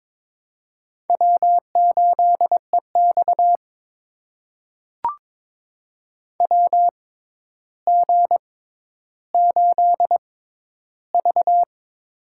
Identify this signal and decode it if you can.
Morse code, heard as a keyed tone, about 22 wpm, spaced as for 11 wpm, 700 Hz. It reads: W8EX WG8V
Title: USA Call Signs - 4 Characters - 6x Character Spacing - 22wpm